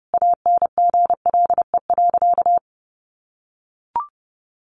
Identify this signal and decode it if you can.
Morse code, heard as a keyed tone, about 30 wpm, 700 Hz. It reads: ANGLE.
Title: Sentences from Top 1500 Words 30wpm